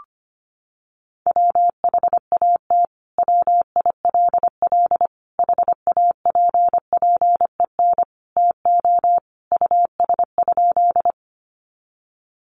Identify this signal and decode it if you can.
Morse code, heard as a keyed tone, about 25 wpm, 700 Hz. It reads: WHAT WILL HAPPEN TO US?
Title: Sentences from Top 400 Words 25wpm